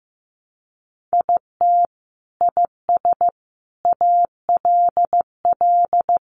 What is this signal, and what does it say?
Morse code, heard as a keyed tone, about 15 wpm, 700 Hz. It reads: IT IS ALL
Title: Sentences from Top 500 Words 15wpm